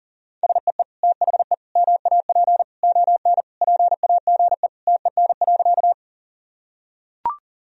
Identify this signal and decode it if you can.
Morse code, heard as a keyed tone, about 40 wpm, 700 Hz. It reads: SEE THE MAP ON PAGE TEN.
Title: Sentences from Top 1000 Words - Review Part 2 - 40wpm